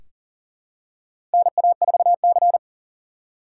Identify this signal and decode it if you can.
Morse code, heard as a keyed tone, about 38 wpm, 700 Hz. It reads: NA4C
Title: K1USN SST QSO Practice - 40wpm